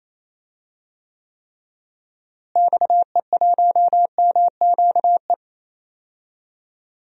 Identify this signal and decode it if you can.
Morse code, heard as a keyed tone, about 28 wpm, 700 Hz. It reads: XE1MQE
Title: DXCC-100 Call Signs - 6 Characters - Repeated 3x - 28wpm